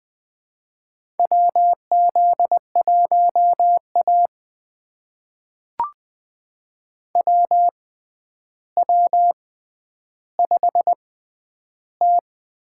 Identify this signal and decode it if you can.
Morse code, heard as a keyed tone, about 20 wpm, spaced as for 10 wpm, 700 Hz. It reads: WZ1A WW5T
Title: USA Call Signs - 6x Character Spacing - 20wpm